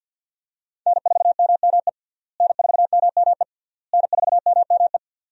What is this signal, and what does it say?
Morse code, heard as a keyed tone, about 46 wpm, 700 Hz. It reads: N4MME N4MME N4MME
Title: USA Call Signs - 5 Characters - Repeated 3x - 50wpm